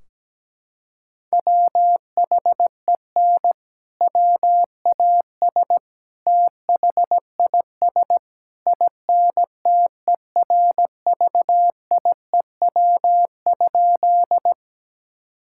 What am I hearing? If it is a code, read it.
Morse code, heard as a keyed tone, about 17 wpm, 700 Hz. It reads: WHEN WAS THIS INTERVIEW?
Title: Sentences from Top 1600 Words 17wpm